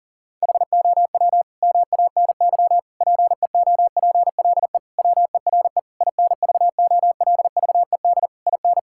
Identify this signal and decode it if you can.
Morse code, heard as a keyed tone, about 40 wpm, 700 Hz. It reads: HOW MANY PEOPLE WERE INVOLVED IN
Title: Sentences from Top 1000 Words - Review Part 5 - 40wpm